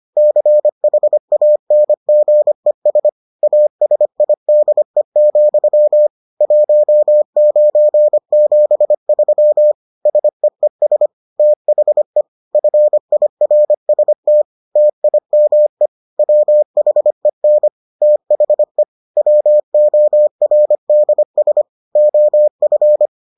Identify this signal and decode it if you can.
Morse code, heard as a keyed tone, about 25 wpm, 600 Hz. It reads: CHANGES ASIDE, 1973 SEES THE FIRST TIME WHEN THE WORDS OF